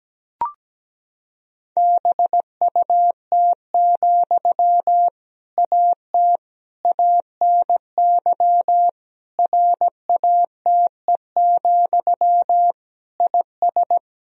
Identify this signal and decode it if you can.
Morse code, heard as a keyed tone, about 17 wpm, 700 Hz. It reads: BUT, AT ANY RATE, IS
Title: Sentences from Top 1000 Words - Review Part 4 - 17wpm